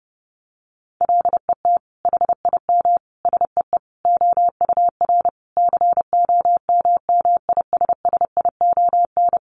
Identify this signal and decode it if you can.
Morse code, heard as a keyed tone, about 30 wpm, 700 Hz. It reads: LET HIM SEE OUR COMMISSION
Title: Sentences from Top 1400 Words 30wpm